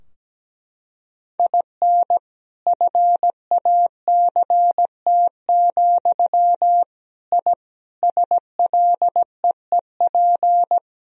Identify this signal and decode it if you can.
Morse code, heard as a keyed tone, about 17 wpm, 700 Hz. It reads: IN FACT, I SLEEP
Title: Sentences from Top 1000 Words - Review Part 3 - 17wpm